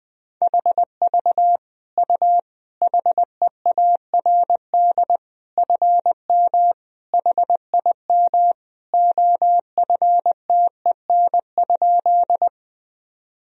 Morse code, heard as a keyed tone, about 21 wpm, 700 Hz. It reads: HV U HEARD FM HIM OFTEN?
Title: Sentences from Top 400 Words and Abbreviations 20wpm